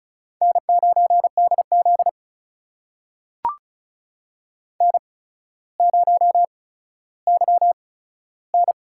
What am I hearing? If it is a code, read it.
Morse code, heard as a keyed tone, about 36 wpm, spaced as for 15 wpm, 700 Hz. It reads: N9DZ N0YN
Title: USA Call Signs - 4 Characters - 8x Character Spacing - 35wpm